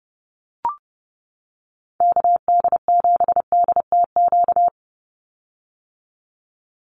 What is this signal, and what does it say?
Morse code, heard as a keyed tone, about 30 wpm, 700 Hz. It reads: KD7DTQ
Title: USA Call Signs 30wpm